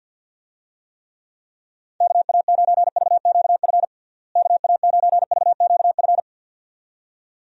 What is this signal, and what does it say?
Morse code, heard as a keyed tone, about 51 wpm, 700 Hz. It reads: KA9VQF KA9VQF
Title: USA Call Signs - 6 Characters - Repeated 2x - 50wpm